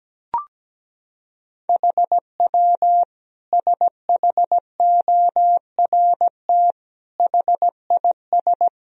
Morse code, heard as a keyed tone, about 17 wpm, 700 Hz. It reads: HW SHORT HIS
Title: Sentences from Top 400 Words and Abbreviations 17wpm